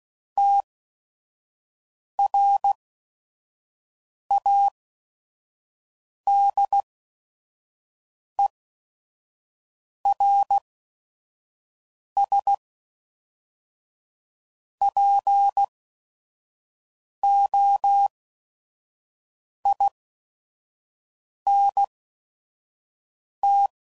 Morse code, heard as a keyed tone, about 16 wpm, spaced as for 5 wpm, 800 Hz. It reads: TRADERSPOINT
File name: CW Code Groups @5WPM / fromSM's / Cities